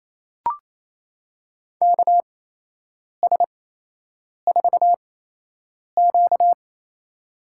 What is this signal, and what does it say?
Morse code, heard as a keyed tone, about 28 wpm, spaced as for 8 wpm, 700 Hz. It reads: KS4Q
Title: USA Call Signs - 4 Characters - 8x Character Spacing - 28wpm